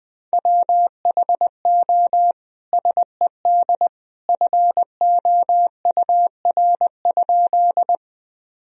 Morse code, heard as a keyed tone, about 21 wpm, 700 Hz. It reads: WHO SED FOUR?
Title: Sentences from Top 300 Words and Abbreviations 20wpm